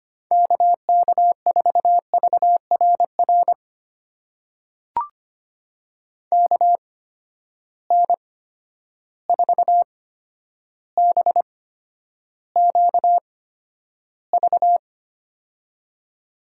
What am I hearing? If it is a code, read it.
Morse code, heard as a keyed tone, about 25 wpm, spaced as for 11 wpm, 700 Hz. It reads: KK4VRR KN4BQV
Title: USA Call Signs - 6 Characters - 8x Character Spacing - 25wpm